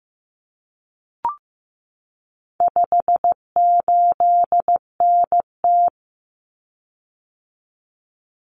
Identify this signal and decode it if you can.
Morse code, heard as a keyed tone, about 15 wpm, 700 Hz. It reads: 58NT
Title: Common Cut Numbers - 4-Digit 15wpm